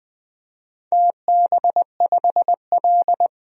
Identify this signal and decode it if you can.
Morse code, heard as a keyed tone, about 20 wpm, 700 Hz. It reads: TB5L
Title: DXCC-100 Call Signs - 4 Characters - 20wpm